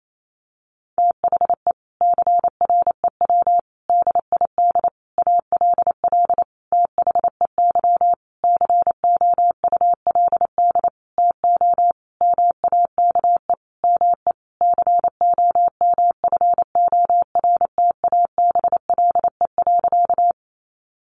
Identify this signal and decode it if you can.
Morse code, heard as a keyed tone, about 28 wpm, 700 Hz. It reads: THE CREW DID ALL THEY COULD TO MAKE ME COMFORTABLE.
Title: Sentences from Top 1500 Words 28wpm